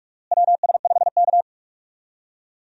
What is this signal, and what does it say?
Morse code, heard as a keyed tone, about 46 wpm, 700 Hz. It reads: WS5K
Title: USA Call Signs - 4 Characters - Repeated 2x - 45wpm